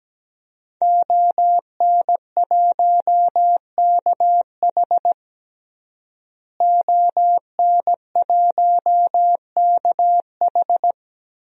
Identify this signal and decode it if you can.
Morse code, heard as a keyed tone, about 17 wpm, 700 Hz. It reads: ON1KH ON1KH
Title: DXCC-100 Call Signs - 5 Characters - Repeated 3x - 17wpm